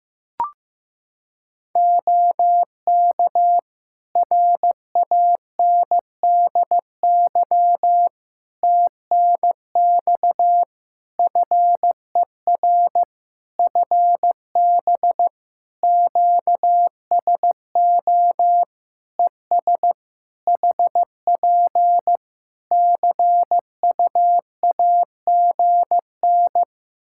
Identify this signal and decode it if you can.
Morse code, heard as a keyed tone, about 15 wpm, 700 Hz. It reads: OK RANDY TNX FER FB QSO ES HP CUAGN
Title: Practice QSO - Exchange 4 - Ending - 15wpm